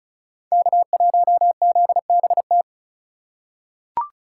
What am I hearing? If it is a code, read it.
Morse code, heard as a keyed tone, about 34 wpm, 700 Hz. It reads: K1ZBT
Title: USA Call Signs - 5 Characters - 4x Character Spacing - 35wpm